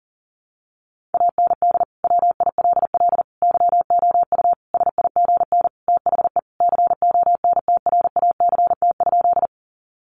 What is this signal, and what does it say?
Morse code, heard as a keyed tone, about 40 wpm, 700 Hz. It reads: AND WILL YOU SIGN THE CONTRACT?
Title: Sentences from Top 1400 Words 40wpm